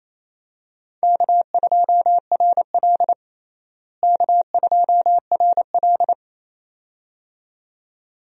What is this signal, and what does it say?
Morse code, heard as a keyed tone, about 28 wpm, 700 Hz. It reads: K2RL K2RL
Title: USA Call Signs - 4 Characters - Repeated 2x - 28wpm